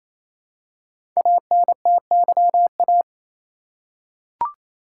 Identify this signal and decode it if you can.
Morse code, heard as a keyed tone, about 28 wpm, 700 Hz. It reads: ANTYA
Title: Code Groups - 5 Letter:Numbers - 28wpm